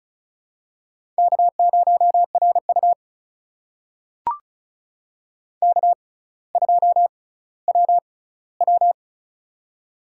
Morse code, heard as a keyed tone, about 36 wpm, spaced as for 18 wpm, 700 Hz. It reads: K0RU K2WW
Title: USA Call Signs - 4 Characters - 6x Character Spacing - 35wpm